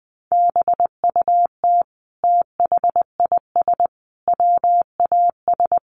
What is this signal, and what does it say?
Morse code, heard as a keyed tone, about 20 wpm, 700 Hz. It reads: BUT THIS WAS
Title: Sentences from Top 1500 Words 20wpm